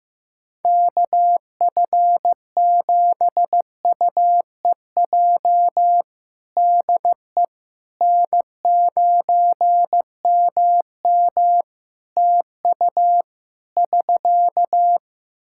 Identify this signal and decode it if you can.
Morse code, heard as a keyed tone, about 15 wpm, 700 Hz. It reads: KF7UEJ DE N9MM TU <SK>
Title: Practice QSO - Exchange 4 - Ending - 15wpm